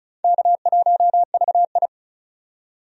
Morse code, heard as a keyed tone, about 36 wpm, 700 Hz. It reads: K1VI
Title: USA Call Signs - 4 Characters - 35wpm